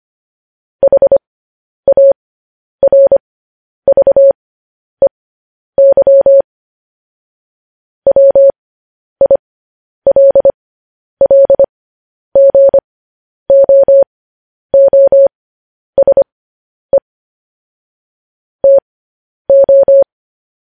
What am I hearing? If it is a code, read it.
Morse code, heard as a keyed tone, about 25 wpm, spaced as for 10 wpm, 550 Hz. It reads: HARVEY WILLGOOSE TO
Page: news headlines in morse code at 10 wpm